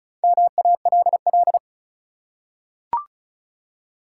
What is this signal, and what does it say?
Morse code, heard as a keyed tone, about 34 wpm, 700 Hz. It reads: MALL
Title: Four Letter Words - 35wpm